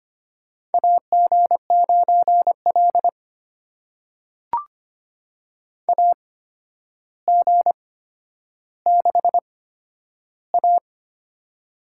Morse code, heard as a keyed tone, about 25 wpm, spaced as for 11 wpm, 700 Hz. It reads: AG9L AG6A
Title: USA Call Signs - 4 Characters - 8x Character Spacing - 25wpm